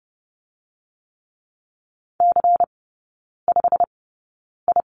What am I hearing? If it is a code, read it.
Morse code, heard as a keyed tone, about 30 wpm, spaced as for 9 wpm, 700 Hz. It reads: C5I
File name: Lesson 16e - Letter c - Three Characters 30wpm